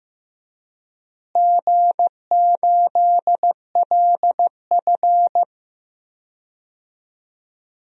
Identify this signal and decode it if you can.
Morse code, heard as a keyed tone, about 15 wpm, 700 Hz. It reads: G8LF
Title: DXCC-100 Call Signs - 4 Characters - Repeated 3x - 15wpm